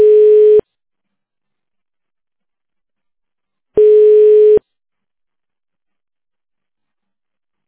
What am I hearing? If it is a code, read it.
Ringback tone (425 Hz, 1 s on, 3 s off).